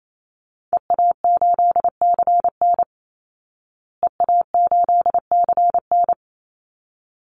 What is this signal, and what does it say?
Morse code, heard as a keyed tone, about 28 wpm, 700 Hz. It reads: EA8CN EA8CN
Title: Call Signs Repeated 2x - 28wpm